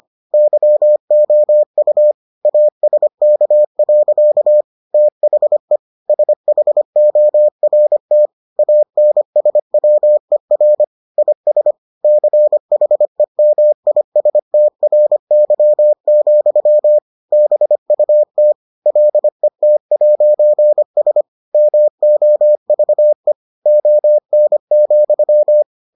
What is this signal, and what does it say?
Morse code, heard as a keyed tone, about 25 wpm, 600 Hz. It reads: YOU ASK. THE SHORT ANSWER IS CHEMISTRY, BUT LET'S MOVE ON,